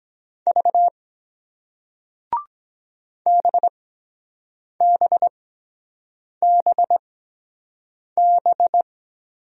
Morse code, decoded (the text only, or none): V BBBB